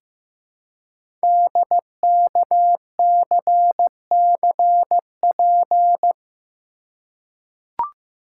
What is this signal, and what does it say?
Morse code, heard as a keyed tone, about 15 wpm, 700 Hz. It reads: DKCCP
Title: Code Groups - 5 Letters - 15wpm